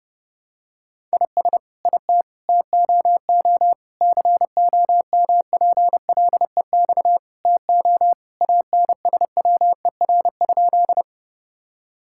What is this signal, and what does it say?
Morse code, heard as a keyed tone, about 32 wpm, 700 Hz. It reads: IS IT TOO COMPLEX TO ANSWER?
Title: Sentences from Top 1000 Words - Review Part 6 - 30wpm